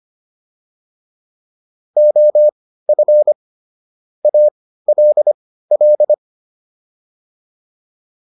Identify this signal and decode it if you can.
Morse code, heard as a keyed tone, about 25 wpm, spaced as for 15 wpm, 600 Hz. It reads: OF ALL